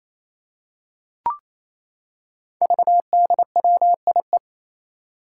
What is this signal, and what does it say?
Morse code, heard as a keyed tone, about 27 wpm, 700 Hz. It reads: VDWIE
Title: Five Challenging Letter-Number Combinations 28wpm